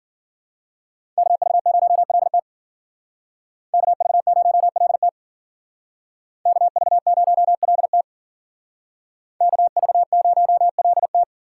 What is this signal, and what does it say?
Morse code, heard as a keyed tone, about 51 wpm, 700 Hz. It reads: KV0LT KV0LT KV0LT KV0LT